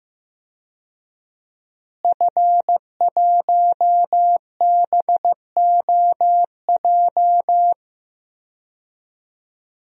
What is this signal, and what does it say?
Morse code, heard as a keyed tone, about 15 wpm, 700 Hz. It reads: F1BOJ
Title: DXCC-100 Call Signs - 5 Characters - Repeated 3x - 15wpm